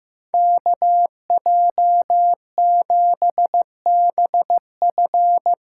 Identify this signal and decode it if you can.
Morse code, heard as a keyed tone, about 15 wpm, 700 Hz. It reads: KJ7BF
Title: USA Call Signs - 6 Characters - 6x Character Spacing - 15wpm